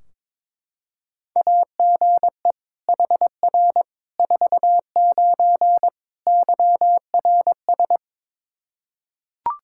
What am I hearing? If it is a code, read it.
Morse code, heard as a keyed tone, about 22 wpm, 700 Hz. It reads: AGE HR 49 YRS
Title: Practice QSO - Exchange 3 - More Info - 22wpm